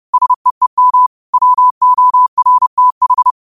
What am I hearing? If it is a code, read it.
Morse code, heard as a keyed tone, about 30 wpm, 1000 Hz. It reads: SEEM WORTH
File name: qotdfiles / 30wpm